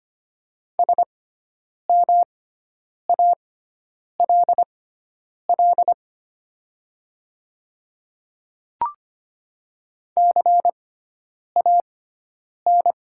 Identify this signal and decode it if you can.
Morse code, heard as a keyed tone, about 25 wpm, spaced as for 9 wpm, 700 Hz. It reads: SMALL CAN
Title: 6x Character Spacing - Top 200 Words - 25wpm